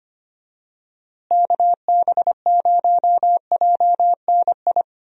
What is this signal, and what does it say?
Morse code, heard as a keyed tone, about 25 wpm, 700 Hz. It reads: KB0JNI
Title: USA Call Signs - 6 Characters - Repeated 2x - 25wpm